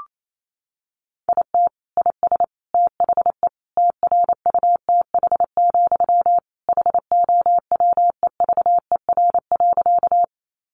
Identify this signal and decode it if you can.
Morse code, heard as a keyed tone, about 28 wpm, 700 Hz. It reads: IT IS THE TRUTH, HOWEVER.
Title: Sentences from Top 800 Words 28wpm